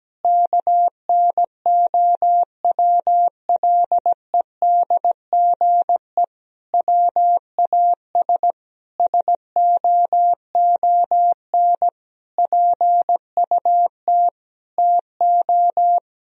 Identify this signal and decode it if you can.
Morse code, heard as a keyed tone, about 17 wpm, 700 Hz. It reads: KNOWLEDGE WAS SOON PUT TO
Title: Sentences from Top 1000 Words - Review Part 3 - 17wpm